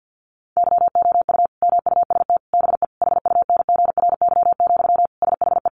Morse code, heard as a keyed tone, about 50 wpm, 700 Hz. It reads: YOU MUST BE HUNGRY, SHE
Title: Sentences from Top 1400 Words 50wpm